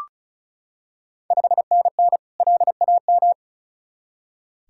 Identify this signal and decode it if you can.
Morse code, heard as a keyed tone, about 36 wpm, 690 Hz. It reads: 5NN LAM
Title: Ontario QSO Party Exchange 35wpm